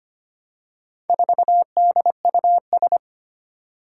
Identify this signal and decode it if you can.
Morse code, heard as a keyed tone, about 25 wpm, 700 Hz. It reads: 4DUS
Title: Four Challenging Letter-Number Combinations 25wpm